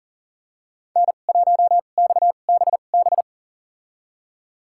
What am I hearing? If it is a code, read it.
Morse code, heard as a keyed tone, about 40 wpm, spaced as for 29 wpm, 700 Hz. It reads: N1XBB
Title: USA Call Signs - 5 Characters - 2x Character Spacing - 40wpm